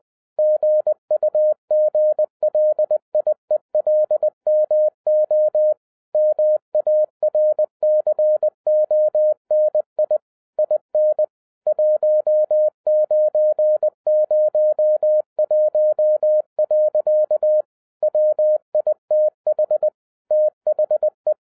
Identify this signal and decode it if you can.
Morse code, heard as a keyed tone, about 20 wpm, 610 Hz. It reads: GUGLIELMO MARCONI IN 1901. WITH THE